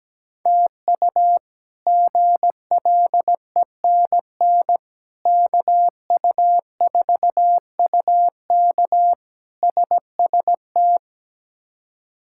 Morse code, heard as a keyed tone, about 17 wpm, 700 Hz. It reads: TU GLENN KU4UK SST
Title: K1USN SST QSO Practice - 17wpm